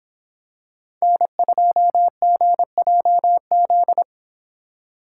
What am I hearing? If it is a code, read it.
Morse code, heard as a keyed tone, about 26 wpm, 700 Hz. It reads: N2GJZ